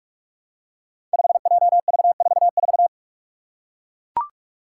Morse code, heard as a keyed tone, about 44 wpm, 700 Hz. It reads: 5JV44
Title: Five Challenging Letter-Number Combinations 45wpm